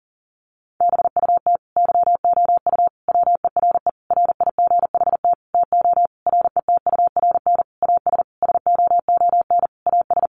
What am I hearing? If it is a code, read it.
Morse code, heard as a keyed tone, about 40 wpm, 700 Hz. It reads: BUT YOU WERE RIGHT TO RETURN AS SOON AS
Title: Sentences from Top 800 Words 40wpm